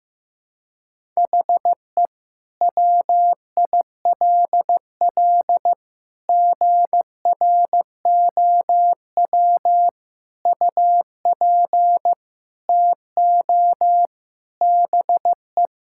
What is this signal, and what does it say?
Morse code, heard as a keyed tone, about 15 wpm, 700 Hz. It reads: HE WILL GROW UP TO BE
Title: Sentences from Top 1000 Words - Review Part 6 - 15wpm